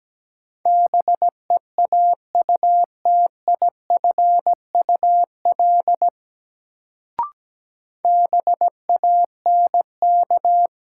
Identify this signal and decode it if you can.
Morse code, heard as a keyed tone, about 17 wpm, 700 Hz. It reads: BEAUTIFUL BANK
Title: Sentences from Top 900 Words - New Word Drill - 17wpm